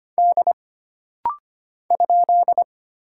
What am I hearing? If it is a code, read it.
Morse code, heard as a keyed tone, about 25 wpm, 700 Hz. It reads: D ?
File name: Single Character (Letter:Number:Punctuation:Prosign) - Rapid-Fire - 25wpm